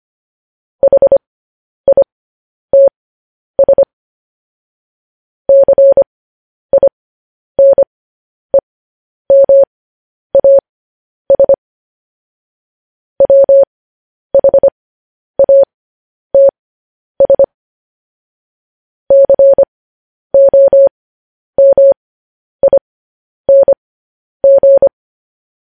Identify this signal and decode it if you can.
Morse code, heard as a keyed tone, about 25 wpm, spaced as for 10 wpm, 550 Hz. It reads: HITS CINEMAS WHATS COMING
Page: news headlines in morse code at 10 wpm